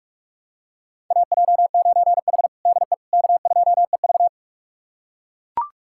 Morse code, heard as a keyed tone, about 44 wpm, 700 Hz. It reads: AJ9H DE K2EV
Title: USA Callsign DE Callsign - 4 Characters - 45wpm